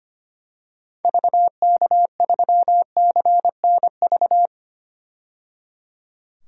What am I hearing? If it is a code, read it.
Morse code, heard as a keyed tone, about 25 wpm, 700 Hz. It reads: VK3CNV
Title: DXCC-100 Call Signs - 6 Characters - 25wpm